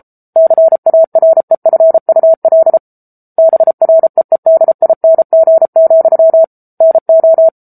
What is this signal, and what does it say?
Morse code, heard as a keyed tone, about 34 wpm, 650 Hz. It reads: CAREFUL BREEDING, NO
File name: Book Two / 07-Chapter 7 - 33 WPM